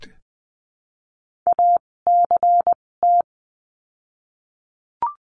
Morse code, heard as a keyed tone, about 20 wpm, spaced as for 16 wpm, 700 Hz. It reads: ACT
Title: 256 Lesson 42a - Words 12-20wpm